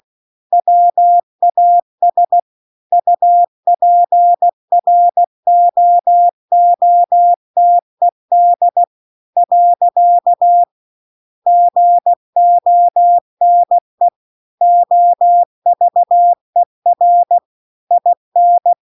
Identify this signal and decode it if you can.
Morse code, heard as a keyed tone, about 16 wpm, 700 Hz. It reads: WAS UPROOTED . GONE OVER IN